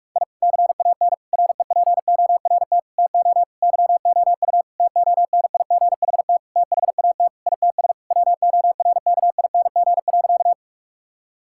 Morse code, heard as a keyed tone, about 41 wpm, 700 Hz. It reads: I CAN REPORT TO YOU TONIGHT THAT ITS WORKING.
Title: Sentences from Top 1000 Words - Review Part 6 - 45wpm